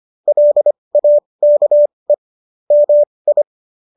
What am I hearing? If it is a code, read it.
Morse code, heard as a keyed tone, about 25 wpm, spaced as for 20 wpm, 590 Hz. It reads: LAKE MI
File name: qso223 20